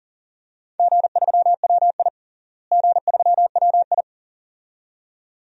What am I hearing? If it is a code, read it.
Morse code, heard as a keyed tone, about 40 wpm, 700 Hz. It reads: G3WI G3WI
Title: DXCC-100 Call Signs - 4 Characters - Repeated 2x - 40wpm